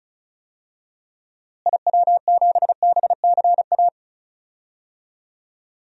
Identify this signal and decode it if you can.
Morse code, heard as a keyed tone, about 36 wpm, 700 Hz. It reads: IW7BCA
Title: DXCC-100 Call Signs - 6 Characters - Repeated 2x - 35wpm